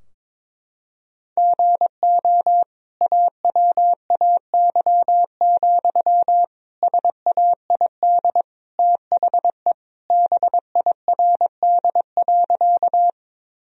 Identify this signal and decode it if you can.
Morse code, heard as a keyed tone, about 22 wpm, 700 Hz. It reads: GO AWAY, SAID THE BIRD.
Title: Sentences from Top 1000 Words - Review Part 2 - 22wpm